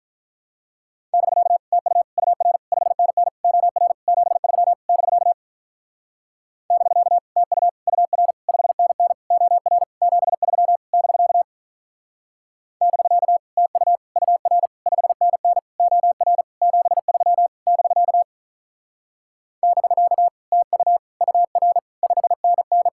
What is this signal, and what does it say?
Morse code, heard as a keyed tone, about 43 wpm, 700 Hz. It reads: <BK> TU UR 5NN OR 73 <BK> <BK> TU UR 5NN OR 73 <BK> <BK> TU UR 5NN OR 73 <BK> <BK> TU UR 5NN
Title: Speed-Racing Common QSO Phrases Encore - 35wpm